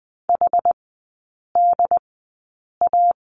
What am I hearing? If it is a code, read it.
Morse code, heard as a keyed tone, about 20 wpm, spaced as for 8 wpm, 700 Hz. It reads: HDA
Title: 087 Lesson 15e - Number 5 - Three Characters 20wpm